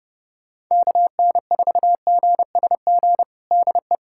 Morse code, heard as a keyed tone, about 30 wpm, 700 Hz. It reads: KN4GSG DE
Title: USA Callsign DE Callsign - 6 Characters - 30wpm